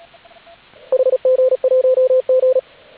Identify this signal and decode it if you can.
Morse code, heard as a keyed tone, about 36 wpm, 520 Hz, with radio noise. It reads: HG1G
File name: AUDIO2019 / HG8DX CW